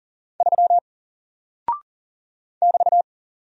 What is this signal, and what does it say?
Morse code, heard as a keyed tone, about 40 wpm, 700 Hz. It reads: 3 <BT>
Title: Single Character (Letter:Number:Punctuation:Prosign) Alt 40wpm